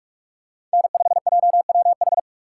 Morse code, heard as a keyed tone, about 45 wpm, 700 Hz. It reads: N5JWH
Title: USA Call Signs - 5 Characters - 3x Character Spacing - 45wpm